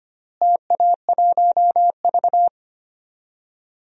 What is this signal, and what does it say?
Morse code, heard as a keyed tone, about 25 wpm, 700 Hz. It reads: TA1V
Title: DXCC-100 Call Signs - 4 Characters - Repeated 3x - 25wpm